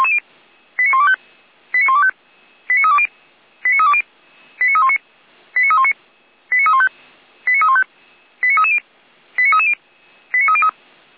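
Five-tone selective call (ZVEI), heard as quick runs of five tones, tones of about 70 ms. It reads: 89125 89125 89320 89320 89319 89319 89315 89315 89330 89330 89303
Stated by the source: ZVEI five-tone selective call received via SDR.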